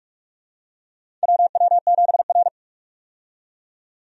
Morse code, heard as a keyed tone, about 42 wpm, 700 Hz. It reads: WW7R
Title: USA Call Signs - 4 Characters - 45wpm